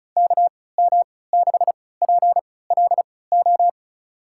Morse code, heard as a keyed tone, about 34 wpm, spaced as for 20 wpm, 700 Hz. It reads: KM6PLO
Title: USA Call Signs - 3x Character Spacing - 35wpm